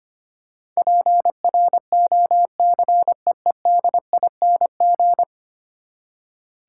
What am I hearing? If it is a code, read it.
Morse code, heard as a keyed tone, about 25 wpm, 700 Hz. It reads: PROCEEDING
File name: Ten Letter Words - 25wpm